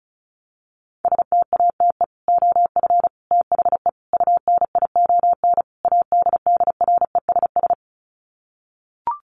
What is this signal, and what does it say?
Morse code, heard as a keyed tone, about 34 wpm, 700 Hz. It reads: STATE OF THE UNION ADDRESS
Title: Sets of 5 Words Encore 35wpm